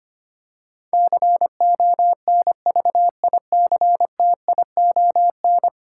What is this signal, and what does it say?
Morse code, heard as a keyed tone, about 25 wpm, 700 Hz. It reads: CONVICTION
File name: Ten Letter Words - 25wpm